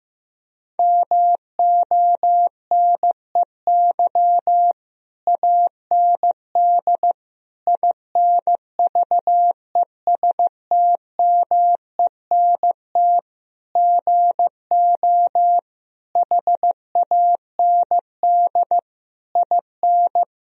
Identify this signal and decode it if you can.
Morse code, heard as a keyed tone, about 15 wpm, 700 Hz. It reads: MONEY AND INVESTMENT GO HAND IN
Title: Sentences from Top 1000 Words - Review Part 4 - 15wpm